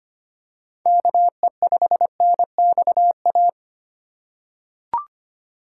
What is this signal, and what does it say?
Morse code, heard as a keyed tone, about 25 wpm, 700 Hz. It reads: KE5NXA
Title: USA Call Signs - 4x Character Spacing - 25wpm